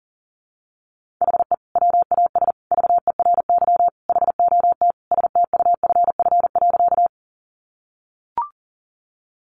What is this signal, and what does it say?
Morse code, heard as a keyed tone, about 40 wpm, 700 Hz. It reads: HE WAS VERY HOT STUFF.
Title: Sentences from Top 1400 Words 40wpm